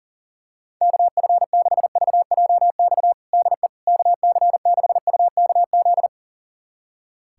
Morse code, heard as a keyed tone, about 40 wpm, 700 Hz. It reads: KF6VJX DE KC6UKZ
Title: USA Callsign DE Callsign - 6 Characters - 40wpm